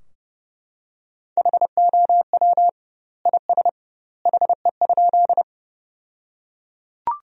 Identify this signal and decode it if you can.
Morse code, heard as a keyed tone, about 29 wpm, 700 Hz. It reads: HOW IS HE?
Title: 2x Word-Spacing - Sentences from Top 100 Words - 30wpm